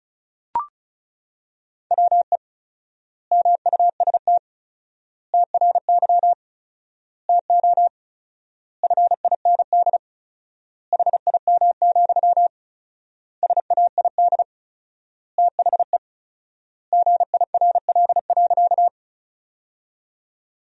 Morse code, heard as a keyed tone, about 36 wpm, 700 Hz. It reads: WE MUST TRY TO FIND HIM, SAID THE GIRL.
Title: 4x Word-Spacing - Sentences from Top 400 Words - 35wpm